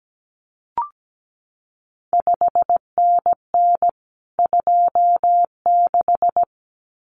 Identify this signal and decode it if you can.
Morse code, heard as a keyed tone, about 17 wpm, 700 Hz. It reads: 5NN 26
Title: CQ WW DX Contest Exchange Encore - 17wpm